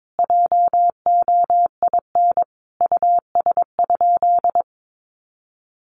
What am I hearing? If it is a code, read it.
Morse code, heard as a keyed tone, about 22 wpm, 700 Hz. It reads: JOIN US?
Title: Sentences from Top 1400 Words 22wpm